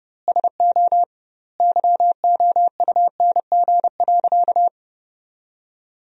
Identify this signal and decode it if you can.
Morse code, heard as a keyed tone, about 30 wpm, 700 Hz. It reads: SO YOUNG.
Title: 2x Word-Spacing - Sentences from Top 400 Words - 30wpm